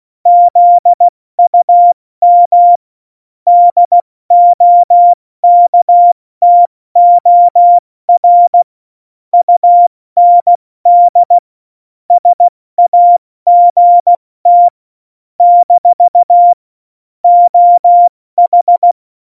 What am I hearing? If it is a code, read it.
Morse code, heard as a keyed tone, about 16 wpm, 700 Hz. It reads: ZUM DOKTOR UND SAGT - OH
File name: Morsekurs / Witze / Witz 09. Januar 2018, 70 F80